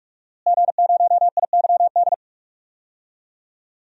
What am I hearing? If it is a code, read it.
Morse code, heard as a keyed tone, about 44 wpm, 700 Hz. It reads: G0IYD
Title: DXCC-100 Call Signs - 5 Characters - Repeated 3x - 45wpm